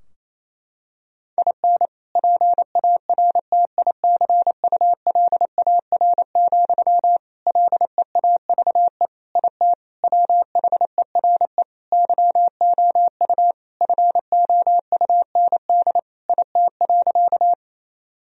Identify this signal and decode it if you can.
Morse code, heard as a keyed tone, about 27 wpm, 700 Hz. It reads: IN PARTICULAR, LEAVE IT WHERE YOU FOUND IT.
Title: Sentences from Top 1000 Words - Review Part 4 - 28wpm